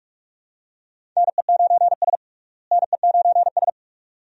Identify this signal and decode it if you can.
Morse code, heard as a keyed tone, about 42 wpm, 700 Hz. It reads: NE9S NE9S
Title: USA Call Signs - 4 Characters - Repeated 2x - 45wpm